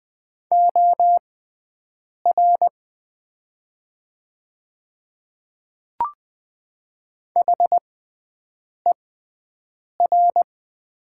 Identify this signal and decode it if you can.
Morse code, heard as a keyed tone, about 20 wpm, spaced as for 7 wpm, 700 Hz. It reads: OR HER
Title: 6x Character Spacing - Top 100 Words - 20wpm